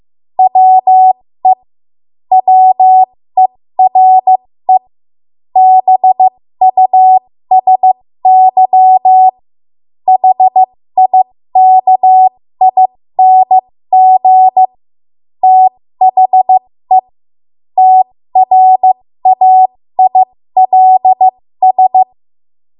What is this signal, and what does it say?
Morse code, heard as a keyed tone, about 15 wpm, spaced as for 13 wpm, 750 Hz. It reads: WE WERE BUSY HIKING THE TRAILS
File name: Archive / 13 WPM / 13WPM